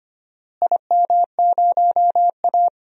Morse code, heard as a keyed tone, about 25 wpm, 700 Hz. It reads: IM0A
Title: DXCC-100 Call Signs - 4 Characters - Repeated 3x - 25wpm